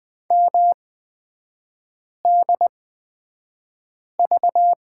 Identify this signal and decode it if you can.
Morse code, heard as a keyed tone, about 20 wpm, spaced as for 5 wpm, 700 Hz. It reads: MDV
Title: 243 Lesson 39e - Letter z - Three Characters 20wpm